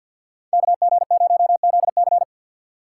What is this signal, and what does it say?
Morse code, heard as a keyed tone, about 52 wpm, 700 Hz. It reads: KG0ZC
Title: USA Call Signs - 5 Characters - 4x Character Spacing - 50wpm